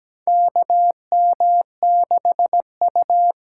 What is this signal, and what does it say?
Morse code, heard as a keyed tone, about 17 wpm, 700 Hz. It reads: KM6U
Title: USA Call Signs - 6 Characters - 8x Character Spacing - 17wpm